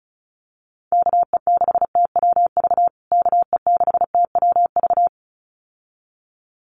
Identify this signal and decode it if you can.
Morse code, heard as a keyed tone, about 34 wpm, 700 Hz. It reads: KE6TWV KE6TWV
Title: USA Call Signs Repeated 2x 35wpm